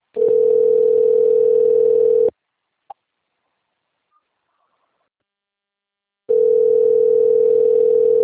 Ringback tone (North America), 440 + 480 Hz.